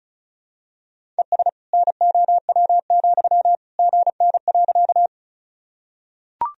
Morse code, heard as a keyed tone, about 34 wpm, 700 Hz. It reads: ES NOW, GN.
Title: Sentences from Top 300 Words and Abbreviations 35wpm